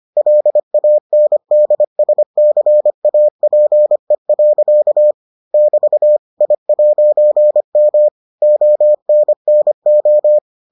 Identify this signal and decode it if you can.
Morse code, heard as a keyed tone, about 25 wpm, 600 Hz. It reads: LANDSCAPE. <BT> I'M ONNO